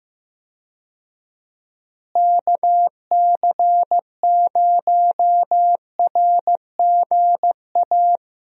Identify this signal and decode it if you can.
Morse code, heard as a keyed tone, about 15 wpm, 700 Hz. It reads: KC0RGA
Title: USA Call Signs - 6 Characters - Repeated 2x - 15wpm